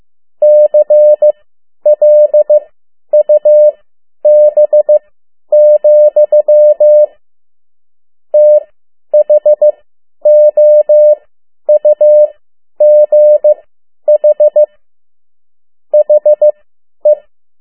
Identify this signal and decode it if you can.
Morse code, heard as a keyed tone, about 15 wpm, spaced as for 10 wpm, 600 Hz. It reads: CLUB, THOUGH HE
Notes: Morse code 10 WPM